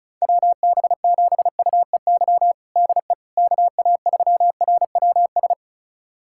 Morse code, heard as a keyed tone, about 36 wpm, 700 Hz. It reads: WB7UEY DE KA3RWS
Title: USA Callsign DE Callsign - 6 Characters - 35wpm